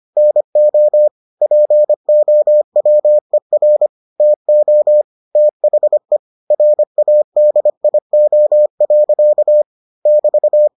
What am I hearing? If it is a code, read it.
Morse code, heard as a keyed tone, about 25 wpm, 600 Hz. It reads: NO POWER TO THE RADIO. <BT>